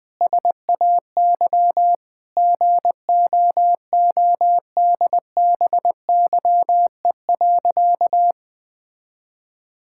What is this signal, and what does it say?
Morse code, heard as a keyed tone, about 20 wpm, 700 Hz. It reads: SAY GOODBYE.
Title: Sentences with 7-Word Phrases - 20wpm